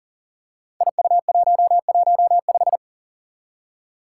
Morse code, heard as a keyed tone, about 38 wpm, 700 Hz. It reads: IU115
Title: Five Challenging Letter-Number Combinations 40wpm